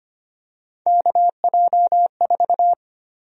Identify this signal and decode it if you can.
Morse code, heard as a keyed tone, about 25 wpm, 700 Hz. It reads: KJ4
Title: USA Call Sign Prefixes - 25wpm